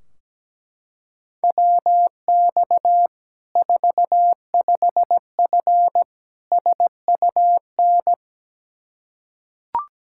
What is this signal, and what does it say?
Morse code, heard as a keyed tone, about 17 wpm, 700 Hz. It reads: WX 45F SUN
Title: Practice QSO - Exchange 2 - Info - 17wpm